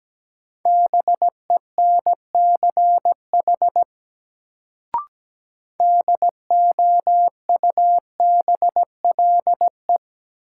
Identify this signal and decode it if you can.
Morse code, heard as a keyed tone, about 17 wpm, 700 Hz. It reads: BENCH DOUBLE
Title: Sentences from Top 1400 Words - New Word Drill Alt - 17wpm